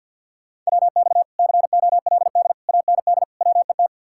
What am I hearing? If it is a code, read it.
Morse code, heard as a keyed tone, about 66 wpm, spaced as for 57 wpm, 700 Hz. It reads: WX COLD AND WET
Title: Speed-Racing Common QSO Phrases - 50wpm